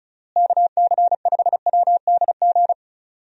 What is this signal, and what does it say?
Morse code, heard as a keyed tone, about 34 wpm, 700 Hz. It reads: KC5WDG
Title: USA Call Signs - 6 Characters - Repeated 3x - 35wpm